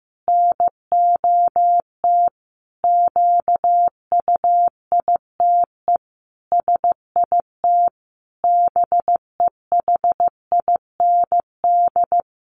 Morse code, heard as a keyed tone, about 15 wpm, 700 Hz. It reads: NOT QUITE SIT BEHIND
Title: A Sound Of Thunder 15wpm